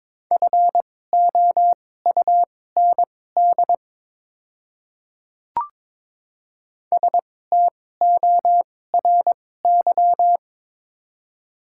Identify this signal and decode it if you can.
Morse code, heard as a keyed tone, about 22 wpm, spaced as for 16 wpm, 690 Hz. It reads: FOUND STORY
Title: 2x Character Spacing - Top 300 Words - 22wpm